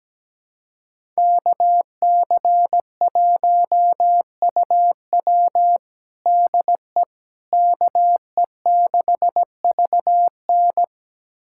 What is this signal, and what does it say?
Morse code, heard as a keyed tone, about 17 wpm, 700 Hz. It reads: KC1UW DE KE6VN
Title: USA Callsign DE Callsign - 5 Characters - 17wpm